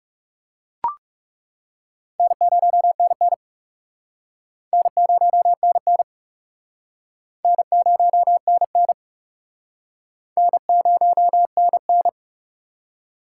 Morse code, heard as a keyed tone, about 39 wpm, 690 Hz. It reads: N0NN N0NN N0NN N0NN